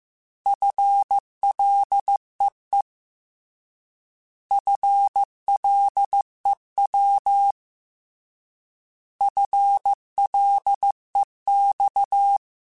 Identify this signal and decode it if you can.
Morse code, heard as a keyed tone, about 15 wpm, 800 Hz. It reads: FLEE FLEW FLEX
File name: Alphabetical / Alphabetical mp3s / Fs @13